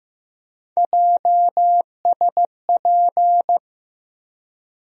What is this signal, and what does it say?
Morse code, heard as a keyed tone, about 15 wpm, 700 Hz. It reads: JSP
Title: USA Call Sign Postfixes - 3 Letters 15wpm